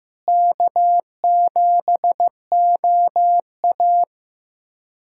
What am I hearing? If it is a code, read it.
Morse code, heard as a keyed tone, about 16 wpm, 700 Hz. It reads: K7OA
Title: USA Callsign DE Callsign - 4 Characters - 15wpm